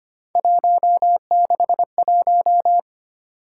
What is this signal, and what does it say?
Morse code, heard as a keyed tone, about 25 wpm, 700 Hz. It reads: 161
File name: Three Letter-Number Combinations - 25wpm